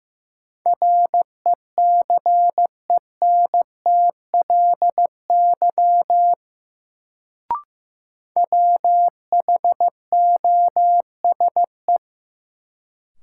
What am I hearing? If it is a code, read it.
Morse code, heard as a keyed tone, about 15 wpm, 700 Hz. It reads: RECENTLY WHOSE
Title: Sentences from Top 1200 Words - New Word Drill - 15wpm